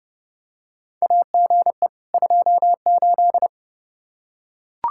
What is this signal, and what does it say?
Morse code, heard as a keyed tone, about 30 wpm, 700 Hz. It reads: AGE 28
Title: Practice QSO - Exchange 3 - More Info - 30wpm